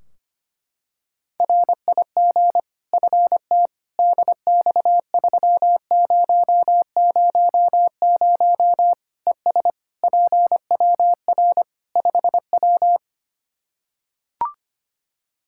Morse code, heard as a keyed tone, about 25 wpm, 700 Hz. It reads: RIG FT DX3000 ES PWR 5W
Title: K1USN SST QSO Practice 25wpm